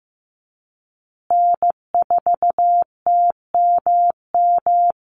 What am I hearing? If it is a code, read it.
Morse code, heard as a keyed tone, about 15 wpm, 700 Hz. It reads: N4TMM
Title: Call Signs Repeated 3x - 15wpm